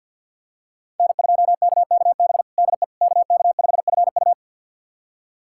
Morse code, heard as a keyed tone, about 50 wpm, 700 Hz. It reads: N2KKB DE KK5FU
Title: USA Callsign DE Callsign 50wpm